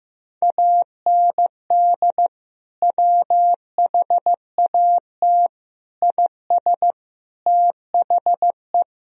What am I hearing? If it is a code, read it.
Morse code, heard as a keyed tone, about 16 wpm, 700 Hz. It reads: AND WHAT IS THE
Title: Sentences from Top 1000 Words - Review Part 3 - 15wpm